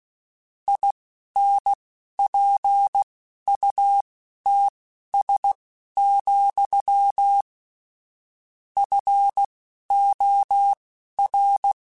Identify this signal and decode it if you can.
Morse code, heard as a keyed tone, about 16 wpm, spaced as for 11 wpm, 800 Hz. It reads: INPUTS, FOR
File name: Technical Subjects / Technical Subjects-mp3s / AND Gate @10